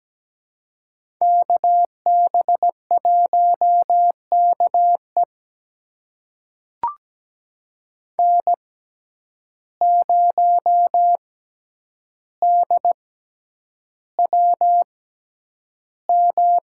Morse code, heard as a keyed tone, about 17 wpm, spaced as for 9 wpm, 700 Hz. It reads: KB1KE N0DWM
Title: USA Call Signs - 5 Characters - 6x Character Spacing - 17wpm